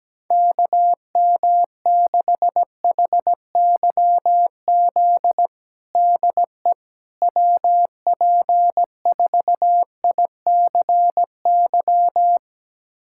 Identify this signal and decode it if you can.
Morse code, heard as a keyed tone, about 17 wpm, 700 Hz. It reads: KM6HYZ DE WP4ICY
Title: USA Callsign DE Callsign - 6 Characters - 17wpm